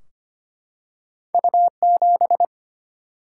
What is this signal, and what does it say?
Morse code, heard as a keyed tone, about 25 wpm, 700 Hz. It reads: U7
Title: Single-Letter Single-Number 25wpm